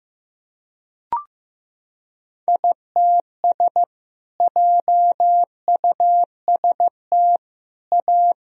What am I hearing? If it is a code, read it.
Morse code, heard as a keyed tone, about 15 wpm, 700 Hz. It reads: ITS JUST A
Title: Sentences from Top 1000 Words - Review Part 4 - 15wpm